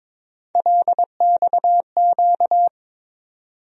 Morse code, heard as a keyed tone, about 22 wpm, 700 Hz. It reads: LXQ
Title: USA Call Sign Postfixes - 3 Letters 22wpm